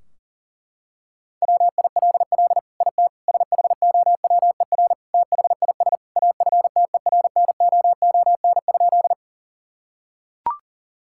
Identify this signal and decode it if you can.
Morse code, heard as a keyed tone, about 40 wpm, 700 Hz. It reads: WILL IT SHOWER THIS AFTERNOON?
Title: Sentences from Top 1600 Words 40wpm